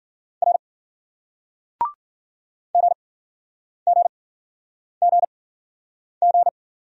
Morse code, heard as a keyed tone, about 53 wpm, spaced as for 9 wpm, 700 Hz. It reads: R GGGG